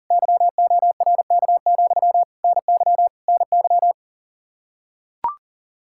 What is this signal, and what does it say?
Morse code, heard as a keyed tone, about 37 wpm, 700 Hz. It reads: YORK, NY NY
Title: Practice QSO - Exchange 1 - Beginning - 40wpm